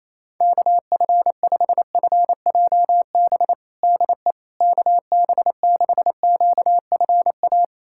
Morse code, heard as a keyed tone, about 28 wpm, 700 Hz. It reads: KF5FJB DE KB6QFA
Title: USA Callsign DE Callsign - 6 Characters - 28wpm